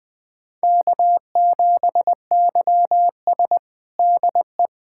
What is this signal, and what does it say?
Morse code, heard as a keyed tone, about 20 wpm, 700 Hz. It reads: K7YS DE
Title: USA Callsign DE Callsign - 4 Characters - 20wpm